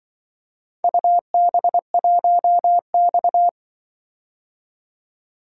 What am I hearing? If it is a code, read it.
Morse code, heard as a keyed tone, about 24 wpm, 700 Hz. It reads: UB1X
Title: Speed-Racing - Four Letter-Number Combinations - 20wpm